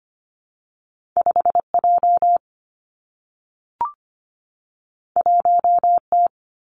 Morse code, heard as a keyed tone, about 25 wpm, 700 Hz. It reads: 5J 1T
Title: Two Challenging Letter-Number Combinations 25wpm